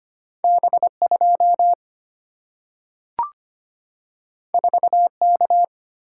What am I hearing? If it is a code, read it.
Morse code, heard as a keyed tone, about 25 wpm, 700 Hz. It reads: B2 4K
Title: Single-Letter Single-Number 25wpm